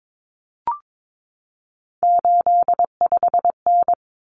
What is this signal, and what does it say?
Morse code, heard as a keyed tone, about 22 wpm, 700 Hz. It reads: 85N
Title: Common Cut Numbers - 3-Digit 22wpm